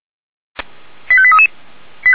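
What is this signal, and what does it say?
Five-tone selective call (ZVEI), heard as quick runs of five tones, tones of about 70 ms. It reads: 85833
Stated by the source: ZVEI five-tone selective call received on SDR.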